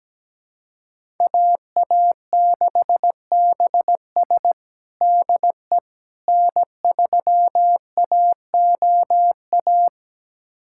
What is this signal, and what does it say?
Morse code, heard as a keyed tone, about 17 wpm, 700 Hz. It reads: AA6BS DE N3AOA
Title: USA Callsign DE Callsign - 5 Characters - 17wpm